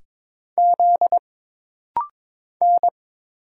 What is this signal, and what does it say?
Morse code, heard as a keyed tone, about 22 wpm, 700 Hz. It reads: Z N
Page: This rapid-fire timing is geared to push your ICR (Instant Character Recognition) skills.